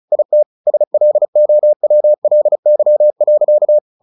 Morse code, heard as a keyed tone, about 34 wpm, 600 Hz. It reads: IT SLOWLY.
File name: the old man and the sea / the old man and the sea 1 35wpm